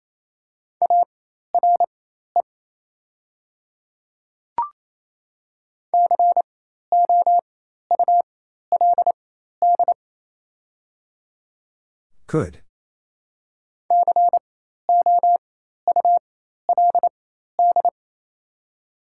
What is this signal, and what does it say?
Morse code, heard as a keyed tone, about 28 wpm, spaced as for 13 wpm, 700 Hz. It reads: ARE COULD COULD
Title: 4x Character Spacing - Top 300 Words - 28wpm